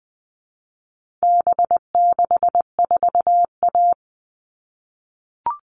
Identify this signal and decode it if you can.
Morse code, heard as a keyed tone, about 20 wpm, 700 Hz. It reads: B64A
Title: All Cut Numbers - 4-Digit 20wpm